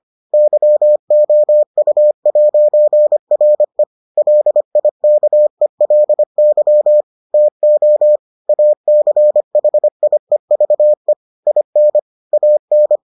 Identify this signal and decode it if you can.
Morse code, heard as a keyed tone, about 25 wpm, 600 Hz. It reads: YOU'RE LIKELY TO ACHIEVE IN AN